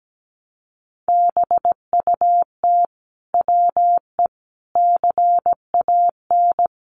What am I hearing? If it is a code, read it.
Morse code, heard as a keyed tone, about 17 wpm, 700 Hz. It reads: BUT WE CAN
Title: Sentences from Top 100 Words 17wpm